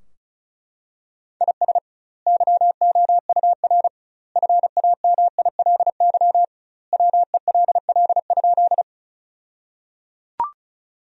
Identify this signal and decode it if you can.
Morse code, heard as a keyed tone, about 36 wpm, 700 Hz. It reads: IS YOUR FAMILY WELL?
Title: 2x Word-Spacing - Sentences from Top 400 Words - 35wpm